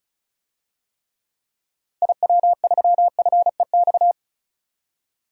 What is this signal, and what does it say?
Morse code, heard as a keyed tone, about 36 wpm, 700 Hz. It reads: IW3FEX
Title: DXCC-100 Call Signs - 6 Characters - 35wpm